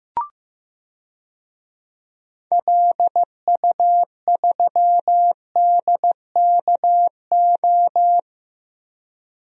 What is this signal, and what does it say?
Morse code, heard as a keyed tone, about 15 wpm, 700 Hz. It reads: LU3DKO
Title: DXCC-100 Call Signs - 6 Characters - 15wpm